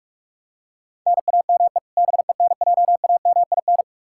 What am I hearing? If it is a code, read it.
Morse code, heard as a keyed tone, about 46 wpm, 700 Hz. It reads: NAME BENJAMIN
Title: K1USN SST QSO Practice 45wpm